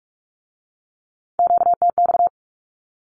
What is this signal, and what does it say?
Morse code, heard as a keyed tone, about 46 wpm, 700 Hz. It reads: QTX
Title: Q-Codes with Spoken Meaning - 45wpm